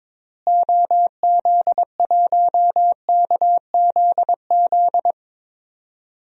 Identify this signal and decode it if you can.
Morse code, heard as a keyed tone, about 22 wpm, 700 Hz. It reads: OZ1KZZ
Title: DXCC-100 Call Signs - 6 Characters - Repeated 2x - 22wpm